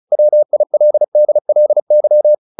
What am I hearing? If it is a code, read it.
Morse code, heard as a keyed tone, about 34 wpm, 600 Hz. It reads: WILDLY
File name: the old man and the sea 2 35wpm